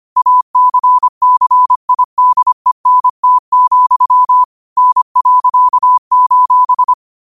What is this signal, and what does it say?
Morse code, heard as a keyed tone, about 25 wpm, 1000 Hz. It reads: ACCIDENT, N.:
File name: qotdfiles / 25wpm